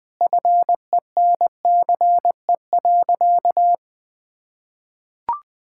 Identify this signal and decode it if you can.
Morse code, heard as a keyed tone, about 21 wpm, 700 Hz. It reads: FENCE.
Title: Sentences with 4-Word Phrases - 20wpm